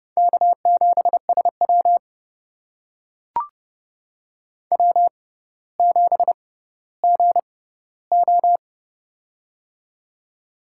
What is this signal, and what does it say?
Morse code, heard as a keyed tone, about 32 wpm, spaced as for 16 wpm, 700 Hz. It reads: K7SW W7GO
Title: USA Call Signs - 6x Character Spacing - 30wpm